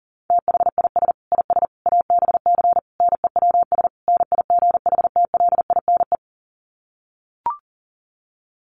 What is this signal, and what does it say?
Morse code, heard as a keyed tone, about 40 wpm, 700 Hz. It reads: THIS IS ABC NEWS NIGHTLINE
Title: Sets of 5 Words 40wpm